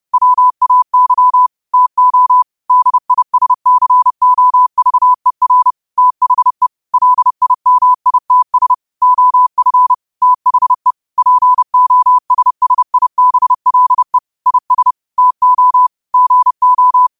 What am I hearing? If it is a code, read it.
Morse code, heard as a keyed tone, about 30 wpm, 1000 Hz. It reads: WAY TO DISCOVER THE LIMITS OF THE POSSIBLE IS TO GO